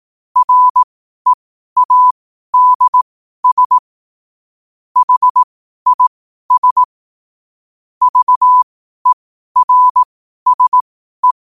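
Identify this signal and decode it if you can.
Morse code, heard as a keyed tone, about 18 wpm, spaced as for 12 wpm, 1000 Hz. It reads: READS HIS VERSE
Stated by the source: Morse code sent at 12 words per minute.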